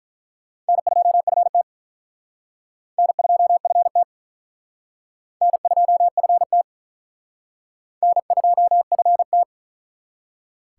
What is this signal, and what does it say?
Morse code, heard as a keyed tone, about 44 wpm, 700 Hz. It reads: N2FT N2FT N2FT N2FT
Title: Speed Racing - USA 1x2 Call Signs 35wpm